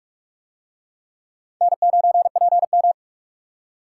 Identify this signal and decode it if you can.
Morse code, heard as a keyed tone, about 42 wpm, 700 Hz. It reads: N9PM